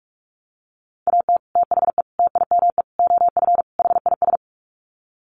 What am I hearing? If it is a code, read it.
Morse code, heard as a keyed tone, about 44 wpm, 700 Hz. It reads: AT THE TIME OF HIS
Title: Sets of 5 Words Encore2 45wpm